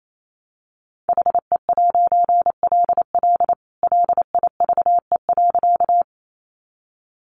Morse code, heard as a keyed tone, about 28 wpm, 700 Hz. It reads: HE'LL LIVE.
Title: A Sound Of Thunder 28wpm